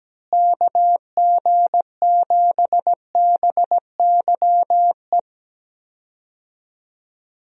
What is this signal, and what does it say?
Morse code, heard as a keyed tone, about 17 wpm, 700 Hz. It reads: KG7BYE
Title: USA Call Signs - 6 Characters - Repeated 2x - 17wpm